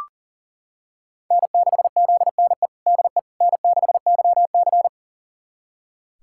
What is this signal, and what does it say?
Morse code, heard as a keyed tone, about 40 wpm, 700 Hz. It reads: N6ZNE DE N6YC
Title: USA Callsign DE Callsign 40wpm